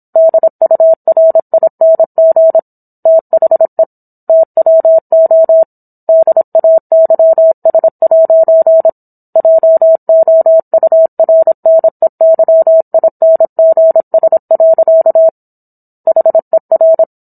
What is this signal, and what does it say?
Morse code, heard as a keyed tone, about 27 wpm, 650 Hz. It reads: DURING THE TWO DAYS' JOURNEYINGS. HER